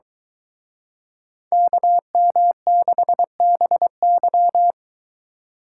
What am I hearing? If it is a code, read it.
Morse code, heard as a keyed tone, about 23 wpm, 700 Hz. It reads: KM6BY